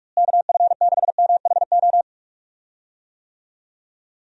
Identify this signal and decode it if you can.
Morse code, heard as a keyed tone, about 47 wpm, 700 Hz. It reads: KF6MHO
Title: USA Call Signs - 6 Characters - Repeated 2x - 45wpm